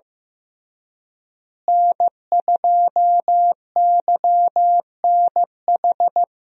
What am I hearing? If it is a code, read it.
Morse code, heard as a keyed tone, about 15 wpm, 700 Hz. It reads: N2YNH
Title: USA Call Signs - 5 Characters - Repeated 3x - 15wpm